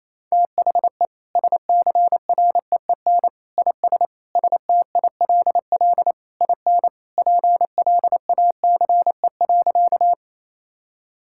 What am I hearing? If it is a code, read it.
Morse code, heard as a keyed tone, about 28 wpm, 700 Hz. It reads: THE SCREEN IS STILL IN PLACE.
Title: Sentences from Top 1000 Words - Review Part 6 - 28wpm